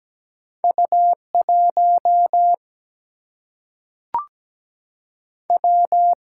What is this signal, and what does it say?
Morse code, heard as a keyed tone, about 17 wpm, 700 Hz. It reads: U 1 W
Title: Single-Letter Single-Number 17wpm